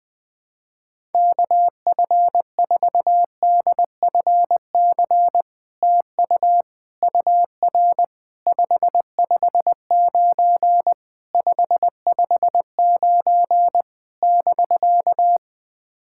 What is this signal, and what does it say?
Morse code, heard as a keyed tone, about 20 wpm, 700 Hz. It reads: KF4DFC TU UR 559 559 <BK>
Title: SOTA QSO Practice 20wpm